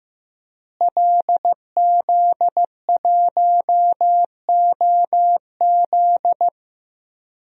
Morse code, heard as a keyed tone, about 15 wpm, 700 Hz. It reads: LZ1OZ
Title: DXCC-100 Call Signs - 5 Characters - Repeated 2x - 15wpm